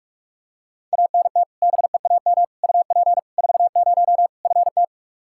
Morse code, heard as a keyed tone, about 46 wpm, 700 Hz. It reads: ANT BEAM UP 40 FT
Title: Practice QSO - Exchange 2 - Info - 45wpm